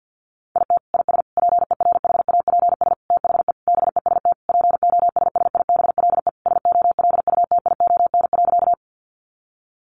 Morse code, heard as a keyed tone, about 50 wpm, 700 Hz. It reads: IT IS PERHAPS THE BEST POSSIBLE SOLUTION.
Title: Sentences from Top 1300 Words 50wpm